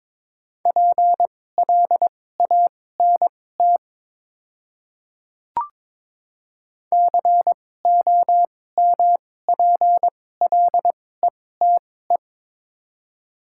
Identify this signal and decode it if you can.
Morse code, heard as a keyed tone, about 22 wpm, spaced as for 16 wpm, 700 Hz. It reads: PLANT COMPLETE
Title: 2x Character Spacing - Top 500 Words - 22wpm